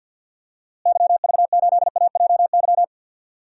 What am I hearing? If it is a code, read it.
Morse code, heard as a keyed tone, about 52 wpm, 700 Hz. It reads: YV8AJY
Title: DXCC-100 Call Signs - 6 Characters - Repeated 3x - 50wpm